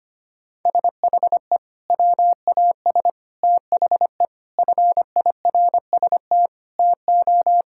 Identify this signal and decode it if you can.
Morse code, heard as a keyed tone, about 25 wpm, 700 Hz. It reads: SHE WAS THE FIRST TO
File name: practice / 25wpm / Sentences with 3-Word Phrases - 25wpm